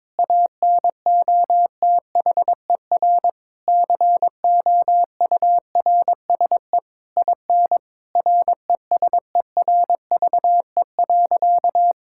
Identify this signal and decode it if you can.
Morse code, heard as a keyed tone, about 22 wpm, 700 Hz. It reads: ANOTHER COURSE IN RESERVE.
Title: Sentences from Top 1000 Words - Review Part 6 - 22wpm